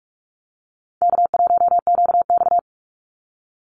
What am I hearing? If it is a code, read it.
Morse code, heard as a keyed tone, about 44 wpm, 700 Hz. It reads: K1QX
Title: Call Signs Repeated 2x - 45wpm